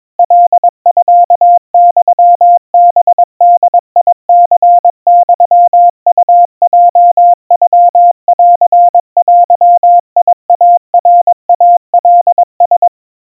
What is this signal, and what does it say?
Morse code (Japanese Wabun code), heard as a keyed tone, about 22 wpm, 700 Hz. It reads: カミユハボニユウヲノンデイナイカラ
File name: 80 cpm 700 Hz Crispy